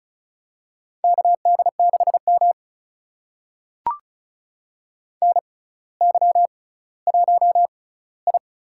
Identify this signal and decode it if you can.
Morse code, heard as a keyed tone, about 36 wpm, spaced as for 18 wpm, 700 Hz. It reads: KD6M NY1I
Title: USA Call Signs - 4 Characters - 6x Character Spacing - 35wpm